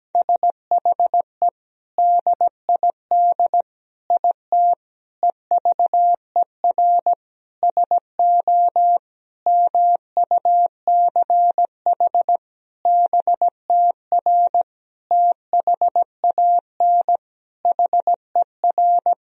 Morse code, heard as a keyed tone, about 17 wpm, 700 Hz. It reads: SHE DID IT EVER SO MUCH BTR THAN HER
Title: Sentences from Top 400 Words and Abbreviations 17wpm